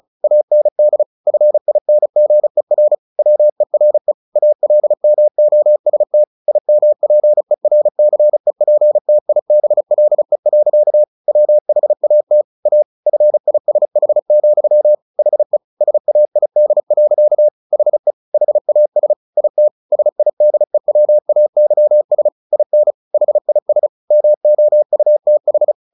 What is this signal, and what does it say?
Morse code, heard as a keyed tone, about 34 wpm, 600 Hz. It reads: AND FINGER WERE ALMOST IMPERCEPTIBLE. WHAT A FISH, HE SAID. HE HAS IT SIDEWAYS IN HIS MOUTH
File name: the old man and the sea / the old man and the sea 1 35wpm